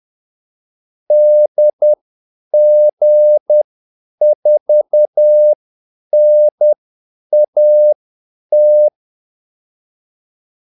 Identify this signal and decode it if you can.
Morse code, heard as a keyed tone, about 10 wpm, spaced as for 8 wpm, 600 Hz. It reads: DG4NAT
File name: Callsigns-10-8wpm